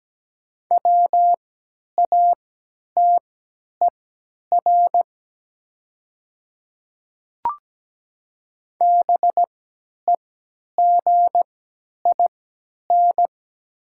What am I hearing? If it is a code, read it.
Morse code, heard as a keyed tone, about 17 wpm, spaced as for 10 wpm, 700 Hz. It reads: WATER BEGIN
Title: 3x Character Spacing - Top 300 Words - 17wpm